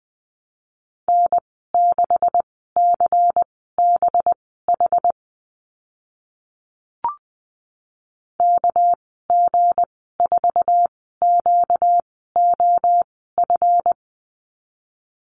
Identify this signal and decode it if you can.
Morse code, heard as a keyed tone, about 20 wpm, spaced as for 14 wpm, 700 Hz. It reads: N6CBH KG4QOF
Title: 238 Lesson 38g - Letter q - Call Signs 20wpm